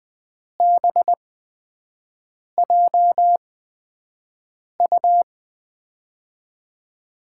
Morse code, heard as a keyed tone, about 20 wpm, spaced as for 5 wpm, 700 Hz. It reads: BJU
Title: USA Call Signs - 6 Characters - 8x Character Spacing - 20wpm